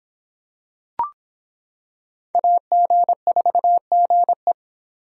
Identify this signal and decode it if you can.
Morse code, heard as a keyed tone, about 26 wpm, 700 Hz. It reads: AG4GE